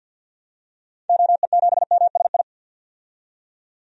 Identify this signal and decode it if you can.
Morse code, heard as a keyed tone, about 47 wpm, 700 Hz. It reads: OE7MSI
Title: DXCC-100 Call Signs - 6 Characters - 50wpm